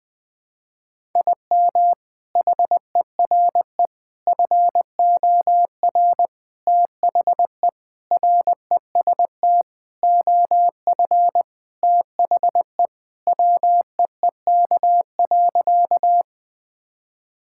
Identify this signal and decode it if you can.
Morse code, heard as a keyed tone, about 20 wpm, 700 Hz. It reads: IM HERE FOR THE REST OF THE WEEK.
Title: Sentences with 4-Word Phrases - 20wpm